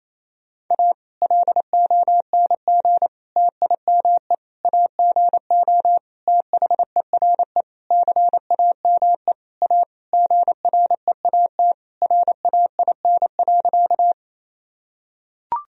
Morse code, heard as a keyed tone, about 28 wpm, 700 Hz. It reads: A LONG TIME AGO THERE CAME A GREAT RAIN.
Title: Sentences from Top 1000 Words - Review Part 3 - 28wpm